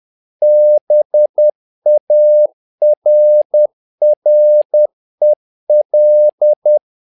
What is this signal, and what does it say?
Morse code, heard as a keyed tone, about 10 wpm, 600 Hz. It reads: BARREL
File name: learning center / tower trouble / tower trouble 10wpm